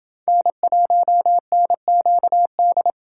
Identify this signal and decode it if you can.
Morse code, heard as a keyed tone, about 29 wpm, 700 Hz. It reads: N1NQD